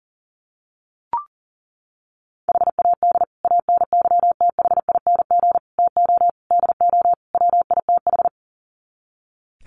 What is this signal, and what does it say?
Morse code, heard as a keyed tone, about 37 wpm, 700 Hz. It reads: HAD ANYTHING TO DO WITH
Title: Sets of 5 Words Encore 40wpm